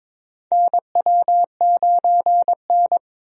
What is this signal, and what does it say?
Morse code, heard as a keyed tone, about 23 wpm, 700 Hz. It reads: NW9N
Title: USA Call Signs - 4 Characters - Repeated 3x - 22wpm